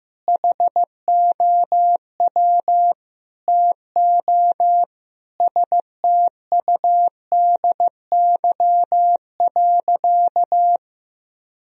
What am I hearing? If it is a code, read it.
Morse code, heard as a keyed tone, about 15 wpm, 700 Hz. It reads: HOW TO STUDY.
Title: Sentences from Top 1000 Words - Review Part 1 - 15wpm